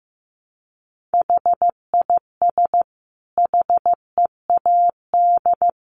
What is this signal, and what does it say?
Morse code, heard as a keyed tone, about 15 wpm, 700 Hz. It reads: HIS HEAD
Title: Sentences from Top 1100 Words 15wpm